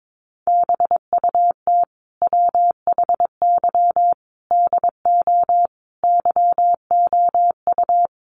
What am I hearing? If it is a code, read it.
Morse code, heard as a keyed tone, about 22 wpm, 700 Hz. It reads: BUT WHY DO YOU
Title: Sentences from Top 200 Words 22wpm